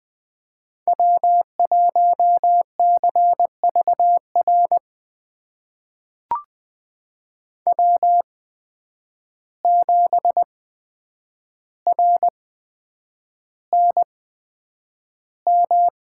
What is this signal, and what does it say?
Morse code, heard as a keyed tone, about 20 wpm, spaced as for 9 wpm, 700 Hz. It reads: W1CVR W7RNM
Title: USA Call Signs - 5 Characters - 8x Character Spacing - 20wpm